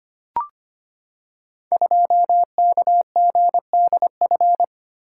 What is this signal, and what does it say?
Morse code, heard as a keyed tone, about 25 wpm, 700 Hz. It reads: 2KGDF
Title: Code Groups - 5 Letter:Numbers - 25wpm